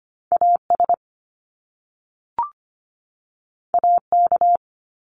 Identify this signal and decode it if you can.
Morse code, heard as a keyed tone, about 25 wpm, 700 Hz. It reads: AS AK
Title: Top Two Letter Combinations in Words 25wpm